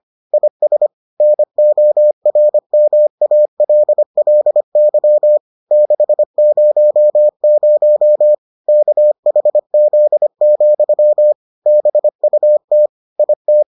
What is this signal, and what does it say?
Morse code, heard as a keyed tone, about 25 wpm, 600 Hz. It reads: IS NORMALLY 600 KHZ, BUT IT